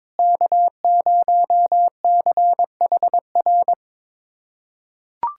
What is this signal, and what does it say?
Morse code, heard as a keyed tone, about 22 wpm, 700 Hz. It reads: K0CHR
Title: USA Callsign DE Callsign - 5 Characters - 22wpm